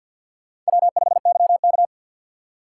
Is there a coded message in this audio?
Yes: Morse code from 0.7 to 1.9 s, heard as a keyed tone, about 64 wpm, 700 Hz. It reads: W5YK